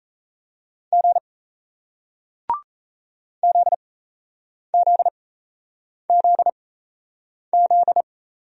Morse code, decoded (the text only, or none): G ZZZZ